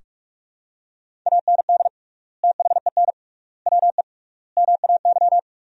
Morse code, heard as a keyed tone, about 41 wpm, 700 Hz. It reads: AND THEN WE MAY
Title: 3x Word-Spacing - Sentences from Top 300 Words - 45wpm